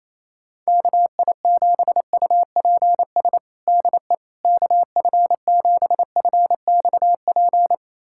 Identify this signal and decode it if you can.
Morse code, heard as a keyed tone, about 28 wpm, 700 Hz. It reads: KI7UPS DE KF7FXP
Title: USA Callsign DE Callsign - 6 Characters - 28wpm